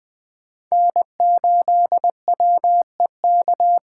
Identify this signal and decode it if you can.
Morse code, heard as a keyed tone, about 20 wpm, 700 Hz. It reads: N8WEK
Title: USA Call Signs - 5 Characters - 4x Character Spacing - 20wpm